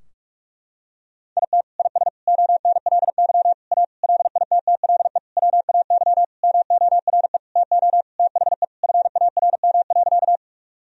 Morse code, heard as a keyed tone, about 41 wpm, 700 Hz. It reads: IT IS ONLY A LITTLE WAY MORE TO THE FARM.
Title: Sentences from Top 1000 Words - Review Part 2 - 45wpm